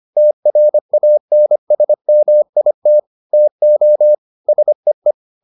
Morse code, heard as a keyed tone, about 25 wpm, 600 Hz. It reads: TRANSMIT TO SEE